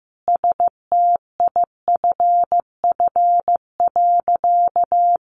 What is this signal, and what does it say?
Morse code, heard as a keyed tone, about 15 wpm, 700 Hz. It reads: STIFF.
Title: A Sound Of Thunder 15wpm